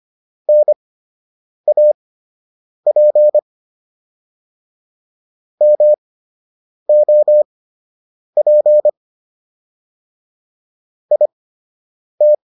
Morse code, heard as a keyed tone, about 25 wpm, spaced as for 8 wpm, 600 Hz. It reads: NAP MOP IT